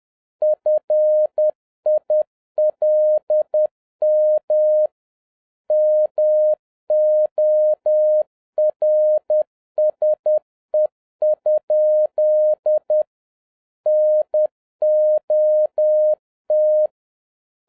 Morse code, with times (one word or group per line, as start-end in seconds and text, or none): 0.4-4.9 FILM
5.7-13.0 MORSE?
13.9-16.9 NOT